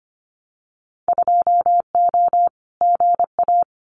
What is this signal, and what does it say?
Morse code, heard as a keyed tone, about 25 wpm, 700 Hz. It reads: 2O GA
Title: Winter Field Day Exchanges Encore 25wpm